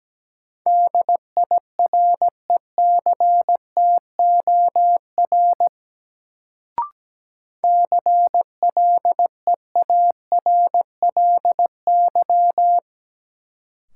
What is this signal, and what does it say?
Morse code, heard as a keyed tone, about 17 wpm, 700 Hz. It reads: DIRECTOR CLEARLY
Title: Sentences from Top 1300 Words - New Word Drill Alt - 17wpm